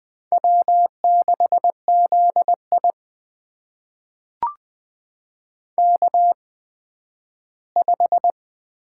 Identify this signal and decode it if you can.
Morse code, heard as a keyed tone, about 20 wpm, spaced as for 12 wpm, 700 Hz. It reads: W6ZI K5
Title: USA Call Signs - 4 Characters - 8x Character Spacing - 20wpm